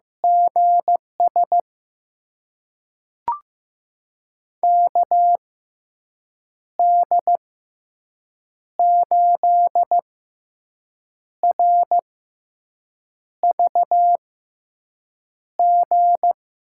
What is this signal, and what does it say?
Morse code, heard as a keyed tone, about 15 wpm, spaced as for 6 wpm, 700 Hz. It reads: GS KD8RVG
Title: USA Call Signs - 6 Characters - 6x Character Spacing - 15wpm